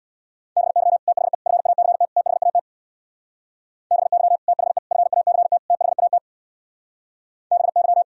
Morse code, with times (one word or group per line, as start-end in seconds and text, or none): 0.6-1.0 BY
1.1-1.3 THE
1.5-2.0 FACT
2.2-2.6 THAT
3.9-4.4 BY
4.5-4.8 THE
4.9-5.6 FACT
5.7-6.2 THAT
7.5-8.0 BY